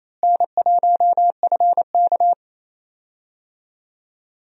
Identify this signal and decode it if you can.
Morse code, heard as a keyed tone, about 28 wpm, 700 Hz. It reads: N1FK
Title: USA Call Signs - 4 Characters - Repeated 3x - 28wpm